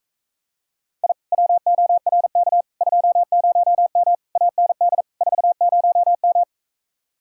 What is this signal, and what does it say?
Morse code, heard as a keyed tone, about 46 wpm, 700 Hz. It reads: I WORK 20M AND 40M